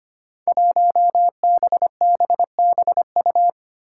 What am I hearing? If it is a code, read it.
Morse code, heard as a keyed tone, about 25 wpm, 700 Hz. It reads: 1BBBU
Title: Five Challenging Letter-Number Combinations 25wpm